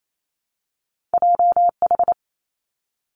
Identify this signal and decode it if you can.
Morse code, heard as a keyed tone, about 28 wpm, 700 Hz. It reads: J H
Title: Two Challenging Letter-Number Combinations 28wpm